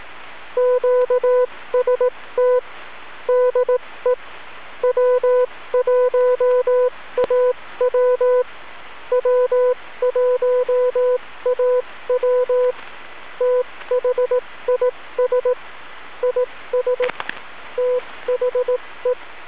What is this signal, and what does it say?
Morse code, heard as a keyed tone, about 18 wpm, spaced as for 15 wpm, 500 Hz. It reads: QST DE W1AW W1AW THIS IS THE
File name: w1aw-fall-80m-snip